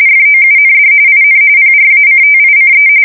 RTTY, about 45.45 baud. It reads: FOUND IN PRACTICE
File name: fsk170 / printing-telegraph